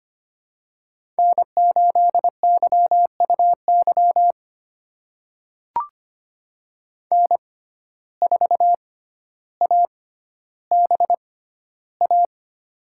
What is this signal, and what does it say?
Morse code, heard as a keyed tone, about 25 wpm, spaced as for 13 wpm, 700 Hz. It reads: N8YUY N4ABA
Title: USA Call Signs - 5 Characters - 6x Character Spacing - 25wpm